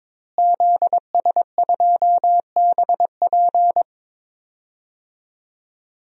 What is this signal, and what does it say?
Morse code, heard as a keyed tone, about 21 wpm, 700 Hz. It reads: ZS2BP
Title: DXCC-100 Call Signs - 5 Characters - Repeated 2x - 22wpm